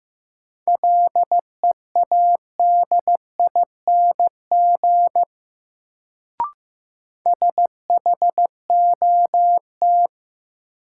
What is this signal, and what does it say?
Morse code, heard as a keyed tone, about 15 wpm, 700 Hz. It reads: LEADING SHOT
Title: Sentences from Top 1100 Words - New Word Drill Alt - 15wpm